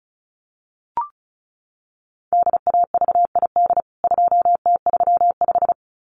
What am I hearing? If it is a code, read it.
Morse code, heard as a keyed tone, about 34 wpm, 700 Hz. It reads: DAVID 2T35
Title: CWT Exchange 35wpm